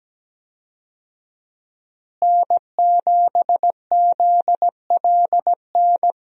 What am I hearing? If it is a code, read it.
Morse code, heard as a keyed tone, about 17 wpm, 700 Hz. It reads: N7ZLN
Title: USA Call Signs - 5 Characters - Repeated 3x - 17wpm